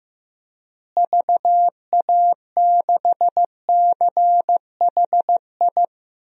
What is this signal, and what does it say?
Morse code, heard as a keyed tone, about 16 wpm, 700 Hz. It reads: VA6CHI
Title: DXCC-100 Call Signs - 6 Characters - Repeated 2x - 15wpm